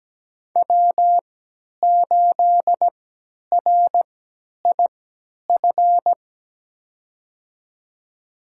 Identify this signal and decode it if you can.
Morse code, heard as a keyed tone, about 17 wpm, spaced as for 10 wpm, 700 Hz. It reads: W8RIF
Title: USA Call Signs - 5 Characters - 3x Character Spacing - 17wpm